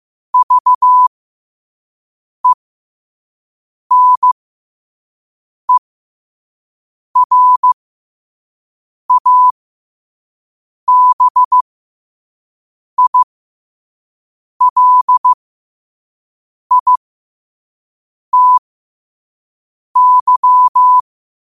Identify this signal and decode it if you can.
Morse code, heard as a keyed tone, about 15 wpm, spaced as for 5 wpm, 1000 Hz. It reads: VENERABILITY
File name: qotdfiles / 5wpm